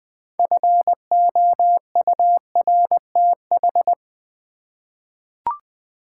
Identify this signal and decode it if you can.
Morse code, heard as a keyed tone, about 20 wpm, 700 Hz. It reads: FOURTH
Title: 6 Letter Words - First 3 Letters Given - 20wpm